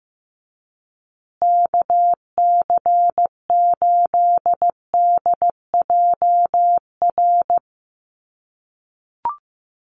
Morse code, heard as a keyed tone, about 15 wpm, 700 Hz. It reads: KC8DJR
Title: USA Call Signs Repeated 2x 15wpm